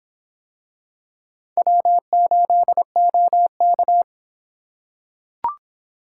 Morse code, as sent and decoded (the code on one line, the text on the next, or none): .-- ---.. --- -.-
W8OK